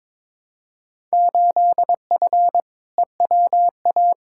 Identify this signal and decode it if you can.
Morse code, heard as a keyed tone, about 22 wpm, 700 Hz. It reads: 8F EWA
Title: Field Day Exchanges Encore - 22wpm